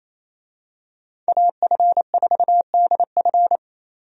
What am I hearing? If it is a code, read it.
Morse code, heard as a keyed tone, about 28 wpm, 700 Hz. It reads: AF4DF
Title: USA Call Signs - 5 Characters - Repeated 2x - 28wpm